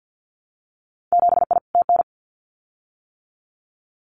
Morse code, heard as a keyed tone, about 50 wpm, 700 Hz. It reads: 7I TN
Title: Winter Field Day Exchanges Encore 50wpm